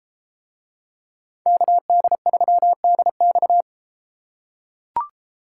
Morse code, heard as a keyed tone, about 33 wpm, 700 Hz. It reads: KD3DX